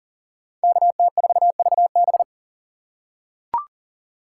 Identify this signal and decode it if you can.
Morse code, heard as a keyed tone, about 40 wpm, 700 Hz. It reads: KT4VB
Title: USA Call Signs - 3x Character Spacing - 40wpm